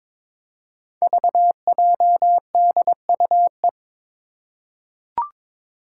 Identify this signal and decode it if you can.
Morse code, heard as a keyed tone, about 22 wpm, 700 Hz. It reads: VJDUE
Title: Five Challenging Letter-Number Combinations 22wpm